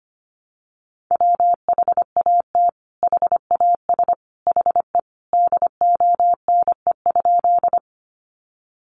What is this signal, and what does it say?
Morse code, heard as a keyed tone, about 25 wpm, 700 Hz. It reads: WHAT HAS HE DONE?
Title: Sentences from Top 500 Words 25wpm